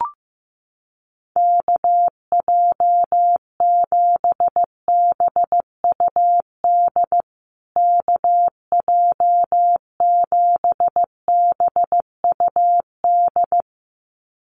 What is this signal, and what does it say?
Morse code, heard as a keyed tone, about 15 wpm, 700 Hz. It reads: KJ7BUD KJ7BUD
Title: USA Call Signs Repeated 2x 15wpm